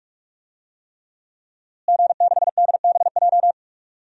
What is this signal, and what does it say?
Morse code, heard as a keyed tone, about 41 wpm, 700 Hz. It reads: G6DBJ
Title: DXCC-100 Call Signs - 5 Characters - Repeated 3x - 45wpm